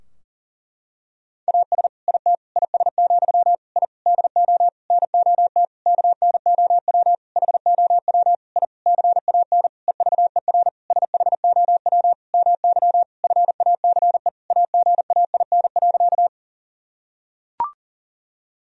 Morse code, heard as a keyed tone, about 40 wpm, 700 Hz. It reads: AS IT IS, I DO NOT KNOW HOW I CAN EVER SHOW MY FACE AGAIN.
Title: Sentences from Top 1000 Words - Review Part 2 - 40wpm